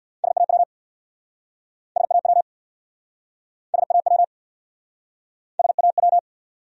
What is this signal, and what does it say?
Morse code, heard as a keyed tone, about 68 wpm, 700 Hz. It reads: SAW SAW SAW SAW